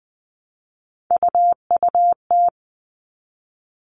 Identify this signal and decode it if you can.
Morse code, heard as a keyed tone, about 20 wpm, 700 Hz. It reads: UUT
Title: All Cut Numbers - 3-Digit 20wpm